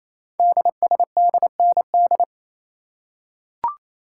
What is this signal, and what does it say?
Morse code, heard as a keyed tone, about 29 wpm, 700 Hz. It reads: DSDND
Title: Code Groups - 5 Letters - 28wpm